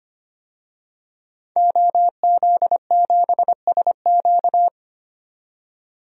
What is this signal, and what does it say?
Morse code, heard as a keyed tone, about 25 wpm, 700 Hz. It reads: OZ7SQ
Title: DXCC-100 Call Signs - 5 Characters - 25wpm